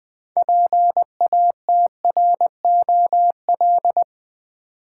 Morse code, heard as a keyed tone, about 20 wpm, 700 Hz. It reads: PATROL
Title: 6 Letter Words - First 3 Letters Given - 20wpm